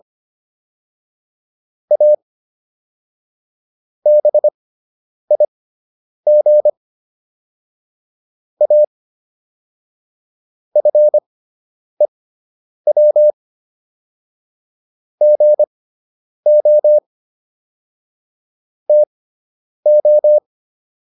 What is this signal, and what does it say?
Morse code, heard as a keyed tone, about 25 wpm, spaced as for 9 wpm, 600 Hz. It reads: A BIG A FEW GO TO